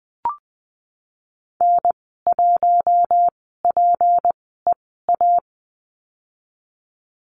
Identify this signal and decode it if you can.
Morse code, heard as a keyed tone, about 20 wpm, spaced as for 15 wpm, 700 Hz. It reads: N1PEA
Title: 207 Lesson 33g - Letter j - Call Signs 20wpm